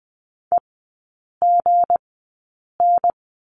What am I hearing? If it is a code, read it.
Morse code, heard as a keyed tone, about 20 wpm, spaced as for 8 wpm, 700 Hz. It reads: EGN
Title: 159 Lesson 26e - Letter g - Three Characters 20wpm